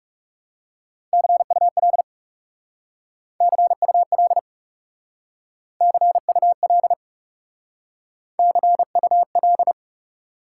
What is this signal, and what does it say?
Morse code, heard as a keyed tone, about 40 wpm, 700 Hz. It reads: CUL CUL CUL CUL